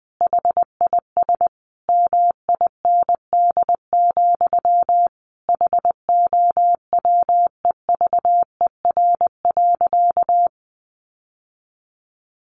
Morse code, heard as a keyed tone, about 20 wpm, 700 Hz. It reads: HIS MIND, HOWEVER.
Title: Sentences from Top 1100 Words 20wpm